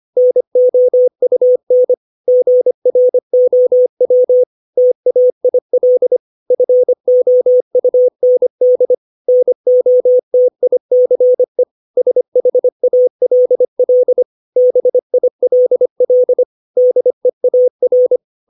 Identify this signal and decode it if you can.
Morse code, heard as a keyed tone, about 25 wpm, 500 Hz. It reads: NOUN GROW TAIL FOUND NOTICE SHALL BILL DEAR
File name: 632 words / 632 Words 25WPM